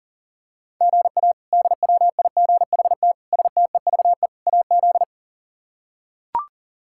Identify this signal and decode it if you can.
Morse code, heard as a keyed tone, about 40 wpm, 700 Hz. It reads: GA DWIGHT STEVE AZ
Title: K1USN SST QSO Practice - 40wpm